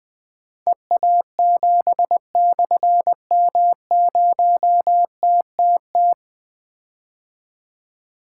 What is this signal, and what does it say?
Morse code, heard as a keyed tone, about 20 wpm, 700 Hz. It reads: EA7/M0TTT
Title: Difficult Call Signs 20wpm